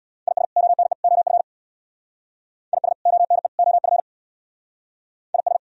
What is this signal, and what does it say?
Morse code, heard as a keyed tone, about 63 wpm, 700 Hz. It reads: IS ONE OF IS ONE OF IS